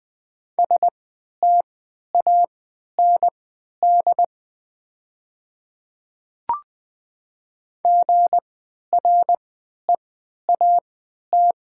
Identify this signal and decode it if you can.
Morse code, heard as a keyed tone, about 20 wpm, spaced as for 11 wpm, 700 Hz. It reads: STAND GREAT
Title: 3x Character Spacing - Top 300 Words - 20wpm